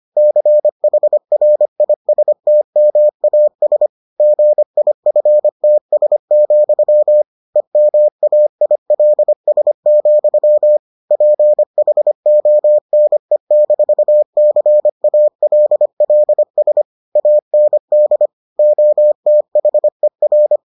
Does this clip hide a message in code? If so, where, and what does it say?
Yes: Morse code from 0.2 to 20.6 s, heard as a keyed tone, about 25 wpm, 600 Hz. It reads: CHRISTMAS GIFTS, EMAILS, PHONE-CALLS AND OTHER